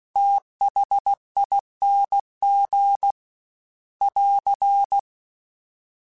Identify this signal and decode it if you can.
Morse code, heard as a keyed tone, about 16 wpm, 800 Hz. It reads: THING <AR>
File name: Special INGs / Special INGs-mp3s / VerySlowToFaster Ing @15